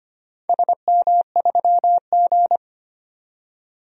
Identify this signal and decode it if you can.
Morse code, heard as a keyed tone, about 25 wpm, 700 Hz. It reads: SM3G
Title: DXCC-100 Call Signs - 4 Characters - 25wpm